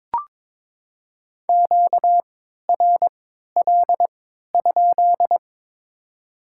Morse code, heard as a keyed tone, about 22 wpm, spaced as for 12 wpm, 700 Hz. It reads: QRL?
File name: Top 90 QSO Elements - 3x Character Spacing - 22wpm